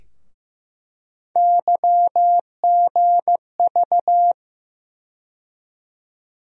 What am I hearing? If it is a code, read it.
Morse code, heard as a keyed tone, about 15 wpm, 700 Hz. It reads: YGV
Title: USA Call Sign Postfixes - 3 Letters 15wpm